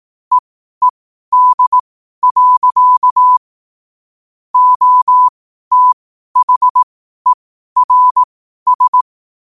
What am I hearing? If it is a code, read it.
Morse code, heard as a keyed tone, about 18 wpm, spaced as for 12 wpm, 1000 Hz. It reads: EED. OTHERS
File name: qotdfiles / 12wpm